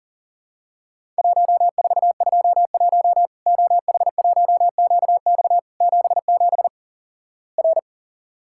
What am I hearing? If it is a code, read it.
Morse code, heard as a keyed tone, about 43 wpm, 700 Hz. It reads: 1421 OH1QX 77 R
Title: Worked All Europe DX Contest - QTC Exchange - 40wpm